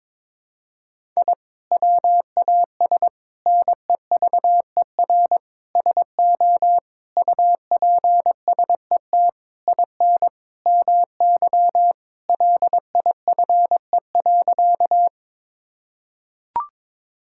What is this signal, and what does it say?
Morse code, heard as a keyed tone, about 22 wpm, 700 Hz. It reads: I WAS NEVER SO UPSET IN MY LIFE.
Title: Sentences from Top 1000 Words - Review Part 5 - 22wpm